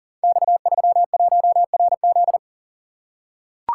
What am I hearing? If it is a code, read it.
Morse code, heard as a keyed tone, about 42 wpm, 700 Hz. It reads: X31RZ
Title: Code Groups - 5 Letter:Numbers - 40wpm